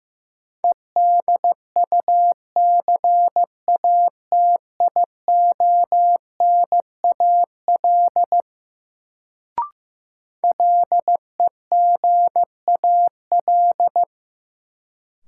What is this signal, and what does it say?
Morse code, heard as a keyed tone, about 15 wpm, 700 Hz. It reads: EDUCATIONAL LEGAL
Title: Sentences from Top 1200 Words - New Word Drill Alt - 15wpm